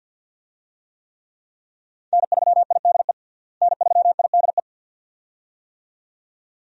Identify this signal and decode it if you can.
Morse code, heard as a keyed tone, about 49 wpm, 700 Hz. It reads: N3IDE N3IDE
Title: USA Call Signs - 5 Characters - Repeated 2x - 50wpm